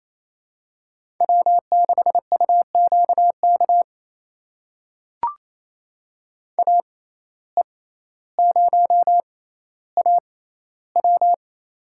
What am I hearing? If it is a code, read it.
Morse code, heard as a keyed tone, about 28 wpm, spaced as for 14 wpm, 700 Hz. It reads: W6UQK AE0AW
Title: USA Call Signs - 5 Characters - 6x Character Spacing - 28wpm